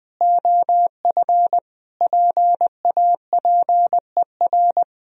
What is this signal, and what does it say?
Morse code, heard as a keyed tone, about 20 wpm, 700 Hz. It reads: OF PAPER
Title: Sentences from Top 400 Words and Abbreviations 20wpm